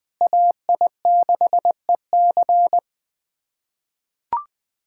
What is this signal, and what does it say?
Morse code, heard as a keyed tone, about 21 wpm, 700 Hz. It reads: AI6EC
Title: USA Callsign DE Callsign - 5 Characters - 20wpm